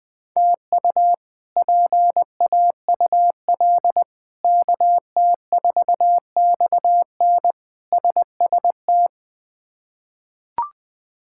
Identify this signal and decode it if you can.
Morse code, heard as a keyed tone, about 20 wpm, 700 Hz. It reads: TU PAUL KT4XN SST
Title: K1USN SST QSO Practice - 20wpm